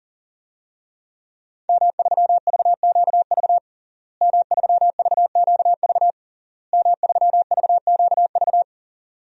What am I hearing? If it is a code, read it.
Morse code, heard as a keyed tone, about 40 wpm, 700 Hz. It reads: M3VQV M3VQV M3VQV
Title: DXCC-100 Call Signs - 5 Characters - Repeated 3x - 40wpm